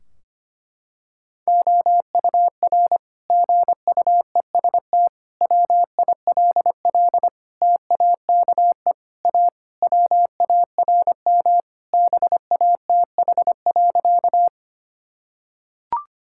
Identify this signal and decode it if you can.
Morse code, heard as a keyed tone, about 25 wpm, 700 Hz. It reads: OUR GUEST WILL TAKE A WARM BATH.
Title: Sentences from Top 1600 Words 25wpm